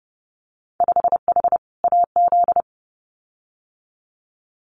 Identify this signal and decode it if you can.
Morse code, heard as a keyed tone, about 30 wpm, 700 Hz. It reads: 5H AZ
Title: Winter Field Day Exchanges Encore 30wpm